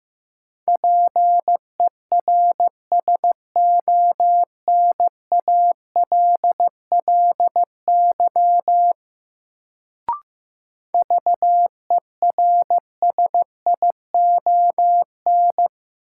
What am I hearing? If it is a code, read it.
Morse code, heard as a keyed tone, about 16 wpm, 700 Hz. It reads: PERSONALLY VERSION
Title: Sentences from Top 1400 Words - New Word Drill Alt - 15wpm